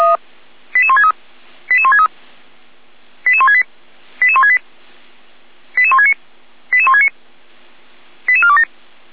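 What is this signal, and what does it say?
Five-tone selective call (ZVEI), heard as quick runs of five tones, tones of about 70 ms. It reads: 88163 88163 88168 88168 88169 88169 88427
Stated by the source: ZVEI five-tone selective call 88203, received on SDR.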